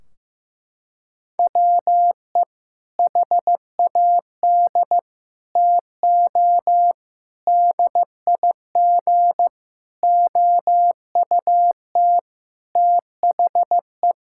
Morse code, heard as a keyed tone, about 15 wpm, 700 Hz. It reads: WE HAD TO DIG OUT THE
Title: Sentences from Top 1600 Words 15wpm